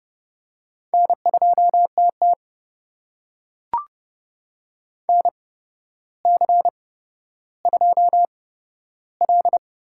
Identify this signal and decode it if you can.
Morse code, heard as a keyed tone, about 30 wpm, spaced as for 13 wpm, 700 Hz. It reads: N2TT NC2L
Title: USA Call Signs - 4 Characters - 8x Character Spacing - 30wpm